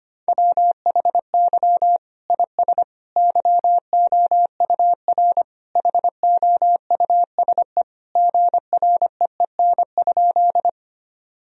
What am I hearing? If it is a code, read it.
Morse code, heard as a keyed tone, about 25 wpm, 700 Hz. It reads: WHY IS YOUR HOUSE GREEN?
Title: Sentences from Top 1000 Words - Review Part 2 - 25wpm